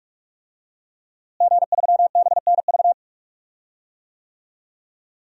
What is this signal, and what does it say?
Morse code, heard as a keyed tone, about 46 wpm, 700 Hz. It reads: G3BNV
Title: DXCC-100 Call Signs - 5 Characters - Repeated 3x - 45wpm